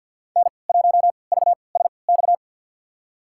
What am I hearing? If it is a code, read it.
Morse code, heard as a keyed tone, about 52 wpm, spaced as for 29 wpm, 700 Hz. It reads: N1VSX
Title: USA Call Signs - 5 Characters - 3x Character Spacing - 50wpm